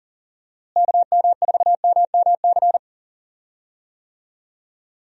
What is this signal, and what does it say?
Morse code, heard as a keyed tone, about 40 wpm, 700 Hz. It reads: KM4MMC
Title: USA Call Signs - 6 Characters - Repeated 2x - 40wpm